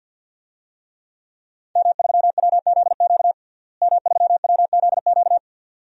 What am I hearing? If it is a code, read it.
Morse code, heard as a keyed tone, about 52 wpm, 700 Hz. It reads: M3WZQ M3WZQ
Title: DXCC-100 Call Signs - 5 Characters - Repeated 2x - 50wpm